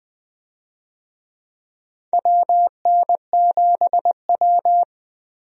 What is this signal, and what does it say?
Morse code, heard as a keyed tone, about 20 wpm, 700 Hz. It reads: WN7W
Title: USA Call Signs - 4 Characters - 20wpm